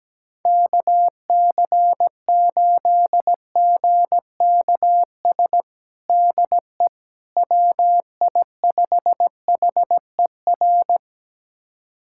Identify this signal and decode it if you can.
Morse code, heard as a keyed tone, about 17 wpm, 700 Hz. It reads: KC8GKS DE WI5HER
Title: USA Callsign DE Callsign 17wpm